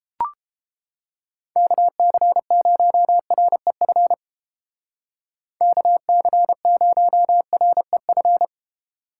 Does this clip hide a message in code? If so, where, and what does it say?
Yes: Morse code from 1.6 to 8.4 s, heard as a keyed tone, about 30 wpm, 700 Hz. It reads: KC0REF KC0REF